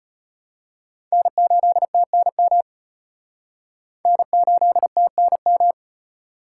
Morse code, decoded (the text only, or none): N8TNM N8TNM